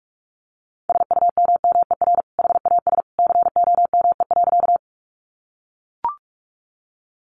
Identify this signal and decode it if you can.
Morse code, heard as a keyed tone, about 44 wpm, 700 Hz. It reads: SUMMER HAS COME.
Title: Sentences from Top 1100 Words 45wpm